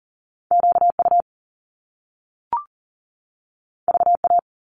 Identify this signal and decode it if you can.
Morse code, heard as a keyed tone, about 41 wpm, 700 Hz. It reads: QU VA
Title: Top Two Letter Combinations in Words 40wpm